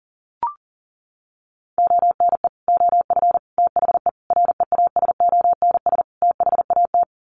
Morse code, heard as a keyed tone, about 37 wpm, 700 Hz. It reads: ONE OF THE REASONS THAT
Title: Sets of 5 Words Encore 40wpm